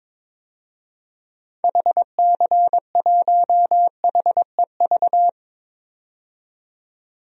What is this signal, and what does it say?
Morse code, heard as a keyed tone, about 22 wpm, 700 Hz. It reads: HC1HEV
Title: DXCC-100 Call Signs - 6 Characters - 22wpm